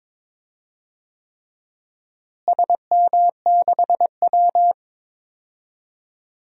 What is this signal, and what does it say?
Morse code, heard as a keyed tone, about 22 wpm, 700 Hz. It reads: SM6W
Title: DXCC-100 Call Signs - 4 Characters - Repeated 3x - 22wpm